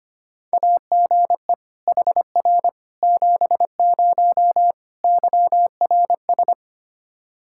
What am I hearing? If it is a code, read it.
Morse code, heard as a keyed tone, about 25 wpm, 700 Hz. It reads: AGE HR 70 YRS
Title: Practice QSO - Exchange 3 - More Info - 25wpm